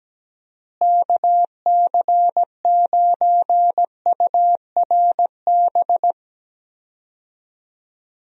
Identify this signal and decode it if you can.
Morse code, heard as a keyed tone, about 17 wpm, 700 Hz. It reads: KC9URB
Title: USA Call Signs - 6 Characters - Repeated 2x - 17wpm